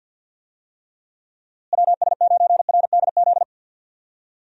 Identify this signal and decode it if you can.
Morse code, heard as a keyed tone, about 51 wpm, 700 Hz. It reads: WS9RDZ
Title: USA Call Signs - 6 Characters - Repeated 2x - 50wpm